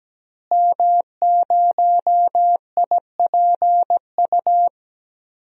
Morse code, heard as a keyed tone, about 17 wpm, 700 Hz. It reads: M0IPU
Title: DXCC-100 Call Signs - 5 Characters - Repeated 2x - 17wpm